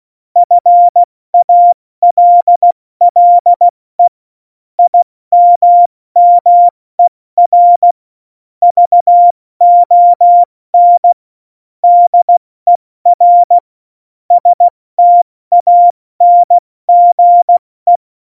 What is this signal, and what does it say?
Morse code, heard as a keyed tone, about 16 wpm, 700 Hz. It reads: FALLE IMMER VON DER STANGE
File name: Morsekurs / Witze / Witz 17 April 2018, 70 F80